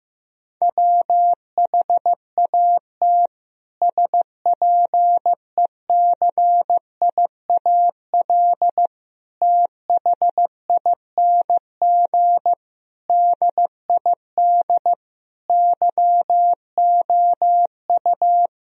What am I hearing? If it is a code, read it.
Morse code, heard as a keyed tone, about 14 wpm, 700 Hz. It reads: WHAT SPECIAL THING DID YOU
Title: Sentences from Top 1000 Words - Review Part 3 - 15wpm